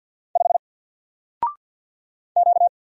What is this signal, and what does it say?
Morse code, heard as a keyed tone, about 50 wpm, 700 Hz. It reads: 5 Q
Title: Single Letter-Number 50wpm